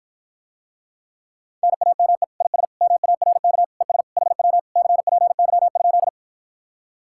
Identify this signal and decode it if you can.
Morse code, heard as a keyed tone, about 53 wpm, 700 Hz. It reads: NAME IS MARK ES HW CPY?